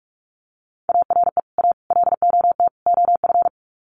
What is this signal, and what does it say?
Morse code, heard as a keyed tone, about 44 wpm, 700 Hz. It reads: ARE A LOT OF
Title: Sets of 4 Words 45wpm